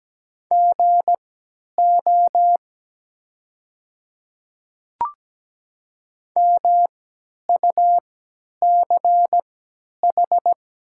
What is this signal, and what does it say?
Morse code, heard as a keyed tone, about 17 wpm, spaced as for 10 wpm, 700 Hz. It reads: GO MUCH
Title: 3x Character Spacing - Top 200 Words - 17wpm